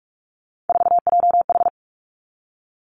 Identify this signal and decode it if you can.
Morse code, heard as a keyed tone, about 46 wpm, 700 Hz. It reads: 4JH
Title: Three Challenging Letter-Number Combinations 45wpm